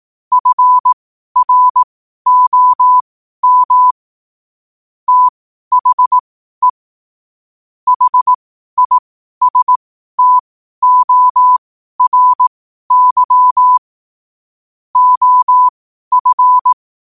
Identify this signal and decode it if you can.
Morse code, heard as a keyed tone, about 18 wpm, spaced as for 12 wpm, 1000 Hz. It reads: FROM THE HISTORY OF
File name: qotdfiles / 12wpm